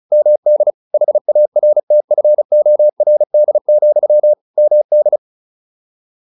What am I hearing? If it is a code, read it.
Morse code, heard as a keyed tone, about 34 wpm, 600 Hz. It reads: MD HARTFORD, MD